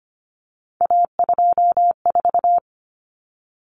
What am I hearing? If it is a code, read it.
Morse code, heard as a keyed tone, about 25 wpm, 700 Hz. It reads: A24
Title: Common Cut Numbers - 3-Digit 25wpm